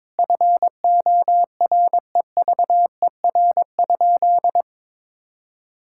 Morse code, heard as a keyed tone, about 22 wpm, 700 Hz. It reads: FOREVER?
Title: Sentences from Top 1600 Words 22wpm